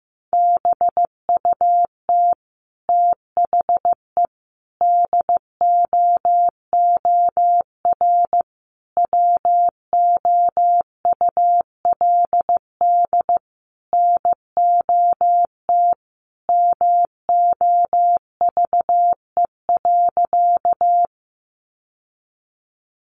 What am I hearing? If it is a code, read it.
Morse code, heard as a keyed tone, about 15 wpm, 700 Hz. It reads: BUT THE DOOR WOULD NOT MOVE.
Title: Sentences from Top 600 Words 15wpm